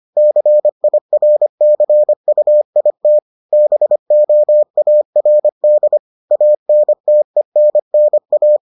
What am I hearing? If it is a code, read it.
Morse code, heard as a keyed tone, about 25 wpm, 600 Hz. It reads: CIRCUIT BOARD ANTENNA